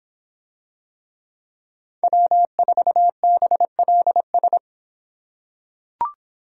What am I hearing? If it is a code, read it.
Morse code, heard as a keyed tone, about 26 wpm, 700 Hz. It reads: W4BLS